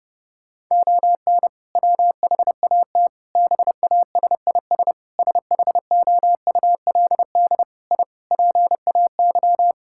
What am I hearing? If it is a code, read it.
Morse code, heard as a keyed tone, about 30 wpm, 700 Hz. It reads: ON WHAT BASIS SHOULD I PAY
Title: Sentences from Top 1000 Words - Review Part 5 - 30wpm